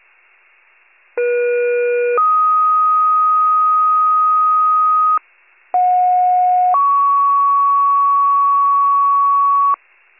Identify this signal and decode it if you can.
Two-tone page, heard as a one-second tone, then a three-second tone, about 480 then 1250 Hz, about 730 then 1120 Hz. It is heard from 1.2 to 9.8 s.
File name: pager / 23 FIRE / wcalert